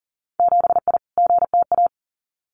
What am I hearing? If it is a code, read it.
Morse code, heard as a keyed tone, about 40 wpm, 700 Hz. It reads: 7I GTA
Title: Winter Field Day Exchanges Encore 40wpm